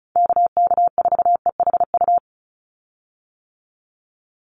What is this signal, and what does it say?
Morse code, heard as a keyed tone, about 34 wpm, 700 Hz. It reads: KK4EHU
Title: USA Call Signs 35wpm